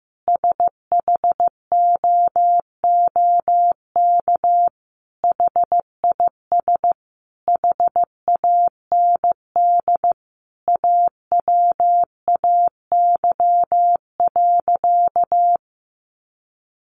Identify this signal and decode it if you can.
Morse code, heard as a keyed tone, about 15 wpm, 700 Hz. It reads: SHOOK HIS HAND AWAY.
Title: A Sound Of Thunder 15wpm